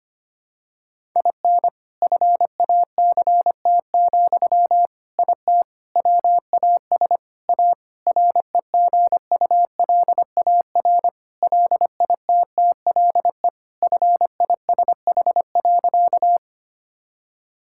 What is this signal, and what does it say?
Morse code, heard as a keyed tone, about 25 wpm, 700 Hz. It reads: IN FACT, IT WAS A REGULAR LITTLE FISH.
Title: Sentences from Top 1000 Words - Review Part 6 - 25wpm